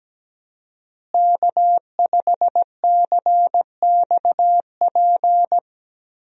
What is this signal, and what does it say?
Morse code, heard as a keyed tone, about 17 wpm, 700 Hz. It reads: K5CXP
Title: USA Call Signs - 5 Characters - Repeated 3x - 17wpm